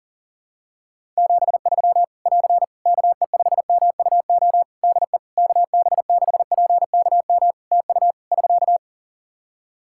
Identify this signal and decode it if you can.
Morse code, heard as a keyed tone, about 40 wpm, 700 Hz. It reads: 73 <AR> KE5MUO DE KB6PKM TU <SK>
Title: Practice QSO - Exchange 4 - Ending - 40wpm